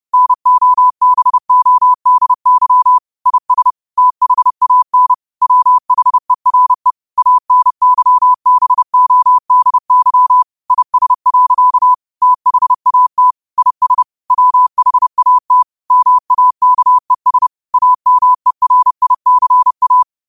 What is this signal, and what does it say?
Morse code, heard as a keyed tone, about 30 wpm, 1000 Hz. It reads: NOBODY IS THAN WHERE ANYBODY IS. THAT IS WHAT MAKES AMERICA